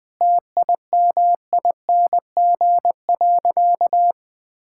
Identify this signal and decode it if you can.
Morse code, heard as a keyed tone, about 20 wpm, 700 Hz. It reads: TIMING.
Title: Sentences with 8-Word Phrases - 20wpm